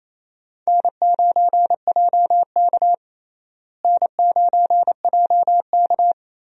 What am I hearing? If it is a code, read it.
Morse code, heard as a keyed tone, about 28 wpm, 700 Hz. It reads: N9JK N9JK
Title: USA Call Signs - 4 Characters - Repeated 2x - 28wpm